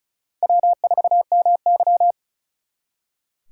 Morse code, heard as a keyed tone, about 44 wpm, 700 Hz. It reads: W4MY